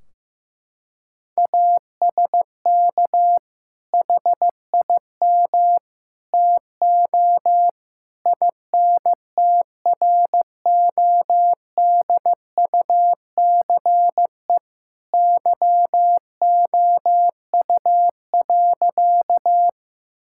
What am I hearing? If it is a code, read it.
Morse code, heard as a keyed tone, about 15 wpm, 700 Hz. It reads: ASK HIM TO INTRODUCE YOU.
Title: Sentences from Top 1600 Words 15wpm